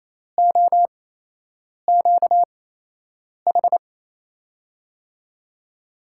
Morse code, heard as a keyed tone, about 27 wpm, spaced as for 8 wpm, 700 Hz. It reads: OQH
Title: USA Call Signs - 6 Characters - 8x Character Spacing - 28wpm